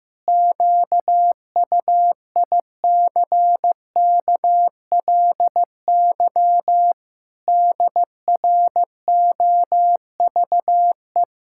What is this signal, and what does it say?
Morse code, heard as a keyed tone, about 14 wpm, 700 Hz. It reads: QUICKLY DROVE
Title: Pangram Challenge v2 - 15wpm